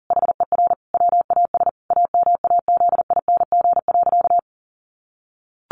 Morse code, heard as a keyed tone, about 40 wpm, 700 Hz. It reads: HER WAS AMAZING.
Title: Sentences from Top 1500 Words 40wpm